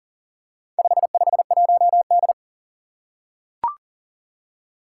Morse code, heard as a keyed tone, about 38 wpm, 700 Hz. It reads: 551D
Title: Four Challenging Letter-Number Combinations 40wpm